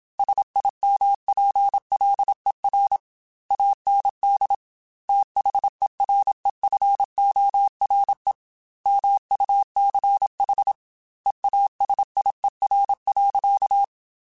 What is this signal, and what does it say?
Morse code, heard as a keyed tone, about 27 wpm, 800 Hz. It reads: SIMPLER AND THEREFORE MUCH EASIER.
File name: Technical Subjects / Technical Subjects-mp3s / AND Gate @25